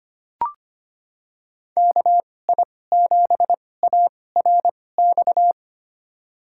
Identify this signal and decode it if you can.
Morse code, heard as a keyed tone, about 25 wpm, spaced as for 18 wpm, 700 Hz. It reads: KI7ARX
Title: USA Call Signs - 6 Characters - 2x Character Spacing - 25wpm